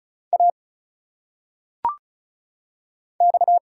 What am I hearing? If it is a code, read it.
Morse code, heard as a keyed tone, about 33 wpm, 700 Hz. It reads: A X
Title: Single Letters - Phonetic Pronunciation 35wpm